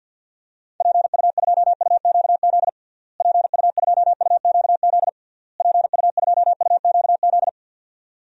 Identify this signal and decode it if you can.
Morse code, heard as a keyed tone, about 52 wpm, 700 Hz. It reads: PU2UQZ PU2UQZ PU2UQZ
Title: DXCC-100 Call Signs - 6 Characters - Repeated 3x - 50wpm